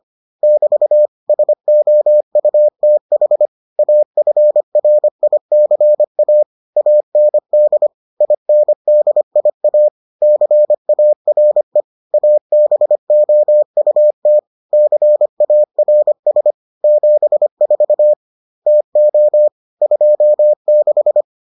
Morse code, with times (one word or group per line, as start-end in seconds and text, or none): 0.4-1.1 <BT>
1.3-3.5 SOUTH
3.8-6.4 AFRICA
6.8-7.9 AND
8.2-9.9 INDIA
10.2-11.8 CARE
12.1-14.4 ABOUT
14.7-16.5 CARS
16.8-18.1 74
18.7-19.5 TO
19.8-21.2 26